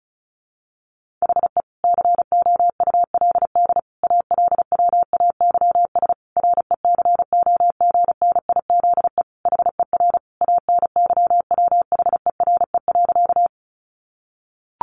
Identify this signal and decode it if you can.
Morse code, heard as a keyed tone, about 34 wpm, 700 Hz. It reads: HE COULD ALWAYS RECOGNIZE HER ANYWHERE.
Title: Sentences from Top 1400 Words 35wpm